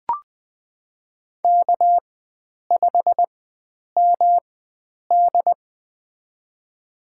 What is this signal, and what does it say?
Morse code, heard as a keyed tone, about 20 wpm, spaced as for 9 wpm, 700 Hz. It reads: K5MD
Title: USA Call Signs - 4x Character Spacing - 20wpm